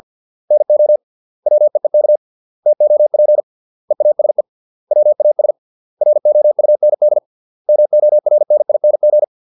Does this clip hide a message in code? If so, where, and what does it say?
Yes: Morse code from 0.5 to 9.2 s, heard as a keyed tone, about 50 wpm, 600 Hz. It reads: NO WEEK TOP EASE WAS ROUND MORNING